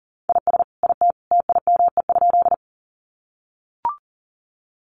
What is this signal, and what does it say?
Morse code, heard as a keyed tone, about 40 wpm, 700 Hz. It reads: IS IT TIME?
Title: Sentences from Top 100 Words 40wpm